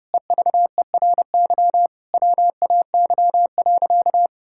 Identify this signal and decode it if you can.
Morse code, heard as a keyed tone, about 32 wpm, 700 Hz. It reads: EVERY WAY.
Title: Sentences from Top 1000 Words - Review Part 6 - 30wpm